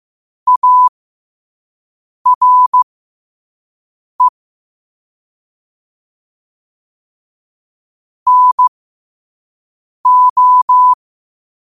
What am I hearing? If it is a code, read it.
Morse code, heard as a keyed tone, about 15 wpm, spaced as for 5 wpm, 1000 Hz. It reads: ARE NO
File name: qotdfiles / 5wpm